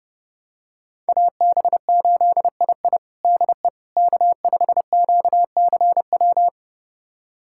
Morse code, heard as a keyed tone, about 30 wpm, 700 Hz. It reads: AB8II DE K5QCW
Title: USA Callsign DE Callsign - 5 Characters - 30wpm